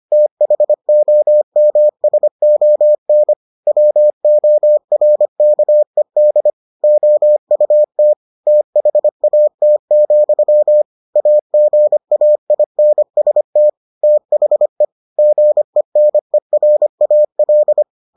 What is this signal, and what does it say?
Morse code, heard as a keyed tone, about 25 wpm, 600 Hz. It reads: THOMSON WORKED OUT THAT, AGAINST THE GENERAL